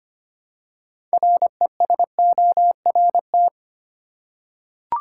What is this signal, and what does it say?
Morse code, heard as a keyed tone, about 25 wpm, 700 Hz. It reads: RESORT
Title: Six Letter Words - 25wpm